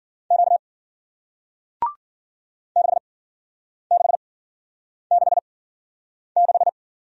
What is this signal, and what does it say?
Morse code, heard as a keyed tone, about 53 wpm, spaced as for 9 wpm, 700 Hz. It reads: Q 6666